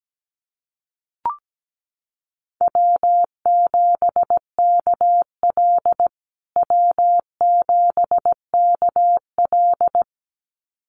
Morse code, heard as a keyed tone, about 17 wpm, 700 Hz. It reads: W7KL W7KL
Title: USA Call Signs Repeated 2x 17wpm